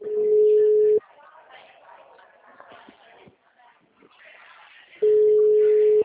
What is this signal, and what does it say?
Ringback tone (Europe, 425 Hz), with speech.
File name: out- -2006- - - .11091